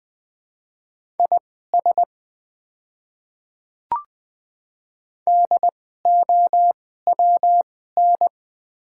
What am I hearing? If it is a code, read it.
Morse code, heard as a keyed tone, about 20 wpm, spaced as for 14 wpm, 700 Hz. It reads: IS DOWN
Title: 2x Character Spacing - Top 300 Words - 20wpm